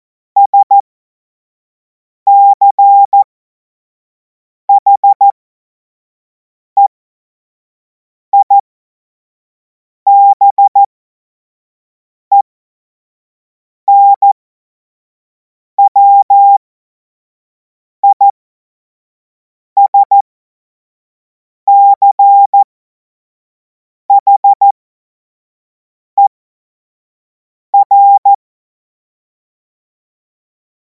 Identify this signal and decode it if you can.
Morse code, heard as a keyed tone, about 14 wpm, spaced as for 5 wpm, 800 Hz. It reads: SCHEIBENWISCHER
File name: Morsekurs / Witze / Witz 23. Juni 2026 25 F70